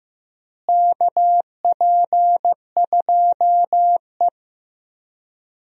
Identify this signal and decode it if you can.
Morse code, heard as a keyed tone, about 15 wpm, 700 Hz. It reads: KP2E
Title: DXCC-100 Call Signs - 4 Characters - Repeated 3x - 15wpm